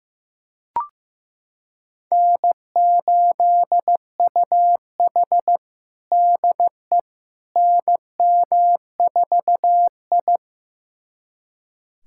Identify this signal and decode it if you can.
Morse code, heard as a keyed tone, about 15 wpm, 700 Hz. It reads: N8UH DE NM4I
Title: USA Callsign DE Callsign - 4 Characters - 15wpm